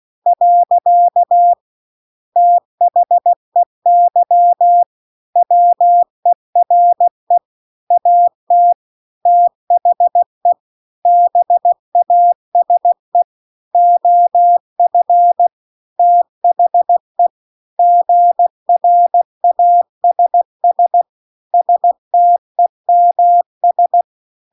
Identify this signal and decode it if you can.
Morse code, heard as a keyed tone, about 16 wpm, 700 Hz. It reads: . THEY WERE AT THE BASE OF THE GRASS STEMS